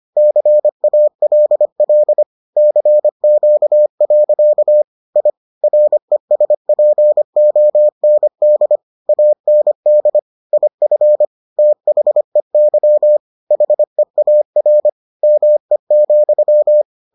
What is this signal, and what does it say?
Morse code, heard as a keyed tone, about 25 wpm, 600 Hz. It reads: CALL CQ. I RESPOND AND IF THEY HEAR ME,